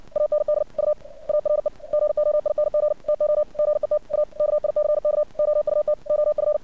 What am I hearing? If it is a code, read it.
Morse code, heard as a keyed tone, about 56 wpm, 600 Hz. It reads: AND U ARE GOING TO GET A KICK OUT OF
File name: archived / CRSnet